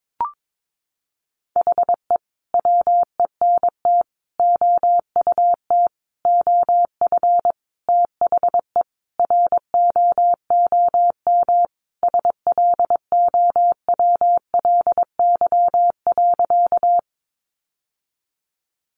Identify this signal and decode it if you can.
Morse code, heard as a keyed tone, about 22 wpm, 700 Hz. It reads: HE WENT OUT OF THE ROOM SLOWLY.
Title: Sentences from Top 1100 Words 22wpm